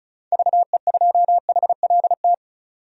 Morse code, heard as a keyed tone, about 36 wpm, 700 Hz. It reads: VE2HLT
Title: DXCC-100 Call Signs - 6 Characters - 35wpm